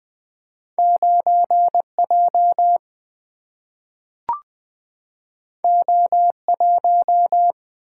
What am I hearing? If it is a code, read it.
Morse code, heard as a keyed tone, about 20 wpm, 700 Hz. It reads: 9 J O 1
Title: Single-Letter Single-Number 20wpm